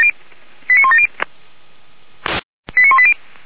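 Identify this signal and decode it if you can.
Five-tone selective call (ZVEI), heard as quick runs of five tones, tones of about 80 ms. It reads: 89170 89180
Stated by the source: Five-tone ZVEI selective call 89110 received by SDR.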